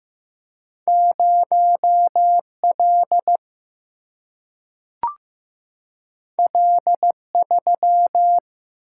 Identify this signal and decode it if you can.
Morse code, heard as a keyed tone, about 15 wpm, 700 Hz. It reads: 0 L L 3
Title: Single-Letter Single-Number 15wpm